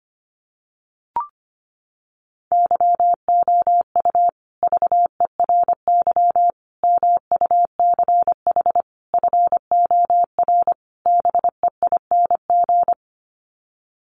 Morse code, heard as a keyed tone, about 25 wpm, 700 Hz. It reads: YOU VERY MUCH FOR BEING
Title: Sets of 5 Words Encore 25wpm